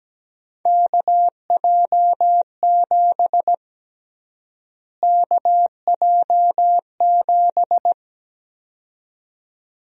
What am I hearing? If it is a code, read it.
Morse code, heard as a keyed tone, about 17 wpm, 700 Hz. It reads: KJ7 KJ7
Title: USA Call Sign Prefixes - Repeated 2x - 17wpm